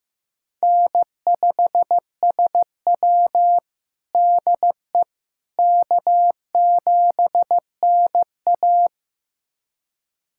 Morse code, heard as a keyed tone, about 15 wpm, 700 Hz. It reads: N5SW DE K7NA
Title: USA Callsign DE Callsign - 4 Characters - 15wpm